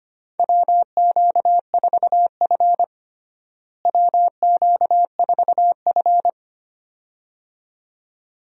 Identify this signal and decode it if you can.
Morse code, heard as a keyed tone, about 25 wpm, 700 Hz. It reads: WQ4F WQ4F
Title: USA Call Signs - 4 Characters - Repeated 2x - 25wpm